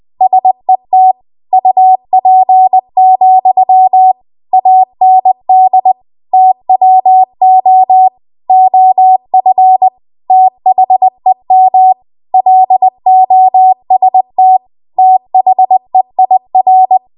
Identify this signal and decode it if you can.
Morse code, heard as a keyed tone, about 20 wpm, 750 Hz. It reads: SET UP, AND TWO OF THEM LOST THEIR